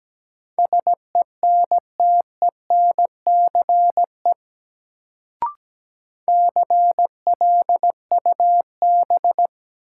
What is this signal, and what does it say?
Morse code, heard as a keyed tone, about 17 wpm, 700 Hz. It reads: SENTENCE CLUB
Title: Sentences from Top 1200 Words - New Word Drill Alt - 17wpm